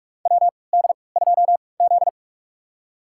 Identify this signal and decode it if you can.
Morse code, heard as a keyed tone, about 46 wpm, spaced as for 25 wpm, 700 Hz. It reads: WD2Z
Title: USA Call Signs - 4 Characters - 3x Character Spacing - 45wpm